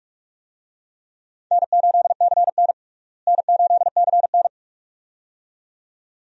Morse code, heard as a keyed tone, about 46 wpm, 700 Hz. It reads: N8CN N8CN
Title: USA Call Signs - 4 Characters - Repeated 2x - 45wpm